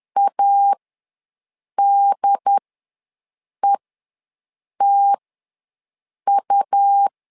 Morse code, heard as a keyed tone, about 10 wpm, spaced as for 6 wpm, 800 Hz. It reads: ADETU
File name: LEZIONI / 2 UHDN / 2 UHD ws 10wpm 2058